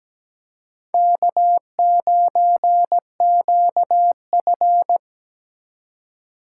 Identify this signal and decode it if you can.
Morse code, heard as a keyed tone, about 17 wpm, 700 Hz. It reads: K9QF
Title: USA Call Signs - 4 Characters - 8x Character Spacing - 17wpm